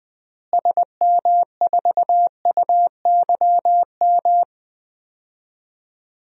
Morse code, heard as a keyed tone, about 20 wpm, 700 Hz. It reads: SM4UYM
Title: DXCC-100 Call Signs - 6 Characters - Repeated 2x - 20wpm